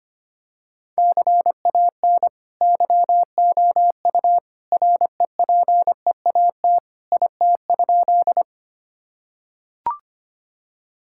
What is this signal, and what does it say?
Morse code, heard as a keyed tone, about 25 wpm, 700 Hz. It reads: CAN YOU REPEAT IT?
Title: Sentences from Top 1600 Words 25wpm